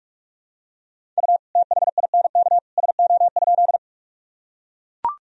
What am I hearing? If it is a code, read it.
Morse code, heard as a keyed tone, about 41 wpm, 700 Hz. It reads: U THINK SO?
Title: Sentences from Top 200 Words and Abbreviations 45wpm